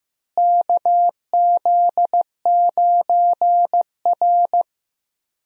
Morse code, heard as a keyed tone, about 15 wpm, 700 Hz. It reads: KZ9R
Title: USA Call Signs - 4 Characters - Repeated 3x - 15wpm